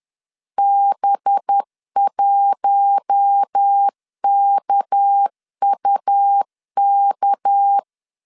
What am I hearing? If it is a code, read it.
Morse code, heard as a keyed tone, about 10 wpm, 800 Hz. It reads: B1KUK
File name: LEZIONI / 6 PKX0 / 6 PK 10wpm 2436